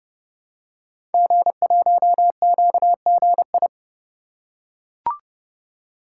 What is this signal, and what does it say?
Morse code, heard as a keyed tone, about 28 wpm, 700 Hz. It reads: G1QGI
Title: Code Groups - 5 Letter:Numbers - 30wpm